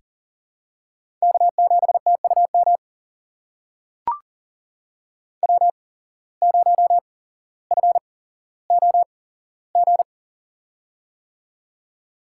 Morse code, heard as a keyed tone, about 40 wpm, spaced as for 17 wpm, 700 Hz. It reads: K7TUM W0FOG
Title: USA Call Signs - 5 Characters - 8x Character Spacing - 40wpm